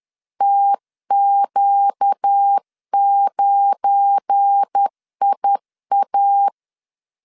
Morse code, heard as a keyed tone, about 10 wpm, 800 Hz. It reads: TQ9IA
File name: LEZIONI / 8 YJ45 / 8 YJ 10wpm 2651